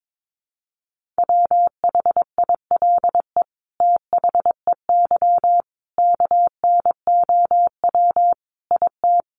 Morse code, heard as a keyed tone, about 22 wpm, 700 Hz. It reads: WHILE THEY KNOW IT
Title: Sentences from Top 1400 Words 22wpm